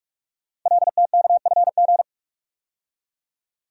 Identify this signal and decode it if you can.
Morse code, heard as a keyed tone, about 44 wpm, 700 Hz. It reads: LTKFG
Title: Code Groups - 5 Letters - 45wpm